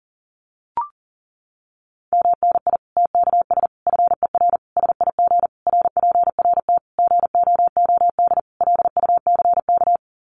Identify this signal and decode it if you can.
Morse code, heard as a keyed tone, about 40 wpm, 700 Hz. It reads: MNI TKS FER SIG RPRT GOOD LUCK
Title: Common QSO Phrases 40wpm